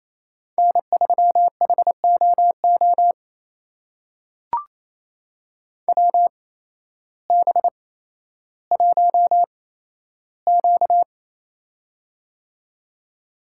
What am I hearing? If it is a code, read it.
Morse code, heard as a keyed tone, about 28 wpm, spaced as for 13 wpm, 700 Hz. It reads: N3HOO WB1Q
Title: USA Call Signs - 8x Character Spacing - 28wpm